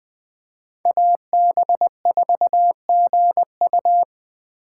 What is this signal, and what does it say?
Morse code, heard as a keyed tone, about 20 wpm, 700 Hz. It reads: AB4GU
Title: USA Call Signs - 5 Characters - 2x Character Spacing - 20wpm